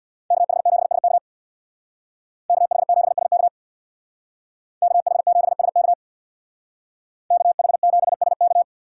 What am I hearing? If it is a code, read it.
Morse code, heard as a keyed tone, about 64 wpm, 700 Hz. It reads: KH7SK KH7SK KH7SK KH7SK